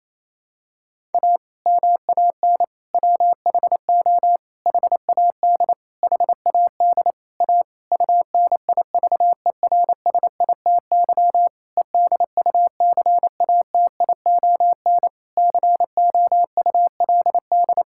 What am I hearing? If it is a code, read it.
Morse code, heard as a keyed tone, about 27 wpm, 700 Hz. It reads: A MAN WHO HAD HAD A UNIVERSITY EDUCATION COULD
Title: Sentences from Top 1000 Words - Review Part 6 - 28wpm